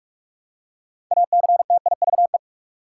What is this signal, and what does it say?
Morse code, heard as a keyed tone, about 46 wpm, 700 Hz. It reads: ACTIVE
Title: Six Letter Words - 45wpm